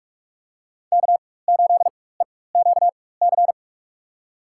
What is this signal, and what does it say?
Morse code, heard as a keyed tone, about 43 wpm, spaced as for 21 wpm, 700 Hz. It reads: K8EQC
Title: USA Call Signs - 5 Characters - 4x Character Spacing - 45wpm